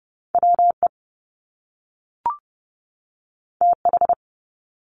Morse code, heard as a keyed tone, about 30 wpm, 700 Hz. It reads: WE TH
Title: Top Two Letter Combinations in Words 30wpm